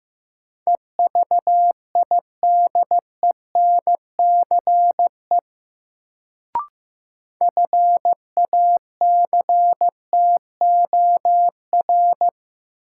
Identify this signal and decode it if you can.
Morse code, heard as a keyed tone, about 15 wpm, 700 Hz. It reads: EVIDENCE FACTOR
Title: Sentences from Top 1300 Words - New Word Drill Alt - 15wpm